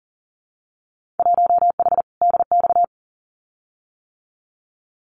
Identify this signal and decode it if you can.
Morse code, heard as a keyed tone, about 42 wpm, 700 Hz. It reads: 1H DX
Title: Winter Field Day Exchanges 40wpm